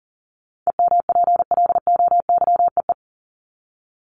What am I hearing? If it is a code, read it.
Morse code, heard as a keyed tone, about 40 wpm, 700 Hz. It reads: EMPLOYEE
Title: Top 1501-1750 Words 40wpm